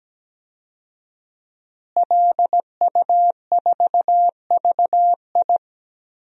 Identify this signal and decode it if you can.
Morse code, heard as a keyed tone, about 17 wpm, 700 Hz. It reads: LU4VI
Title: DXCC-100 Call Signs - 5 Characters - Repeated 2x - 17wpm